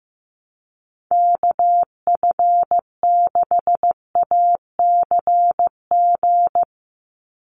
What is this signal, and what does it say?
Morse code, heard as a keyed tone, about 15 wpm, 700 Hz. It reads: KF6ACG
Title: USA Call Signs 15wpm